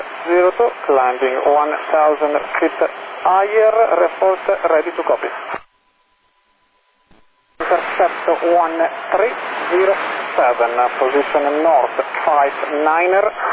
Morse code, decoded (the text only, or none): none